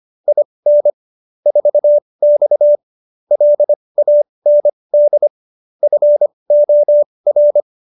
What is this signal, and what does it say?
Morse code, heard as a keyed tone, about 25 wpm, spaced as for 20 wpm, 590 Hz. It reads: IN 4X LAND FOR